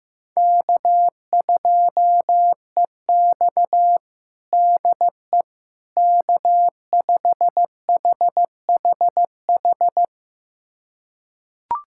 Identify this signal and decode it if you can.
Morse code, heard as a keyed tone, about 15 wpm, 700 Hz. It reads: K2EX DE K5HHH
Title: USA Callsign DE Callsign 15wpm